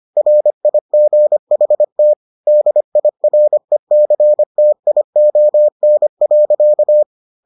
Morse code, heard as a keyed tone, about 25 wpm, 600 Hz. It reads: RIGHT DIRECTION.